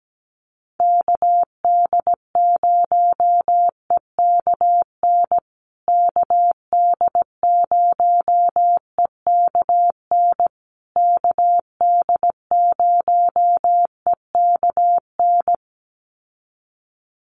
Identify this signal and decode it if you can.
Morse code, heard as a keyed tone, about 17 wpm, 700 Hz. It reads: KD0EKN KD0EKN KD0EKN
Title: USA Call Signs Repeated 3x 17wpm